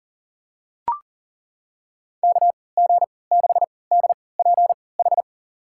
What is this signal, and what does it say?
Morse code, heard as a keyed tone, about 40 wpm, spaced as for 23 wpm, 700 Hz. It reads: KG6DPH
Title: USA Call Signs - 3x Character Spacing - 40wpm